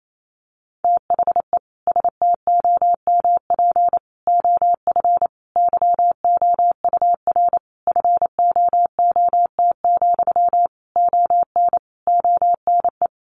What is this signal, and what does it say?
Morse code, heard as a keyed tone, about 28 wpm, 700 Hz. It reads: THE STOMP OF YOUR FOOT, ON ONE
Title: A Sound Of Thunder 28wpm